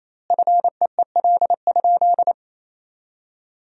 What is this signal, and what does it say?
Morse code, heard as a keyed tone, about 28 wpm, 700 Hz. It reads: FEEL?
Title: 2x Word-Spacing - Sentences from Top 400 Words - 28wpm